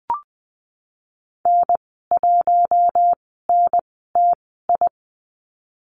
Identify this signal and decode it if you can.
Morse code, heard as a keyed tone, about 20 wpm, spaced as for 15 wpm, 700 Hz. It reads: N1NTI
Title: 055 Lesson 10g - Letter r - Call Signs 20wpm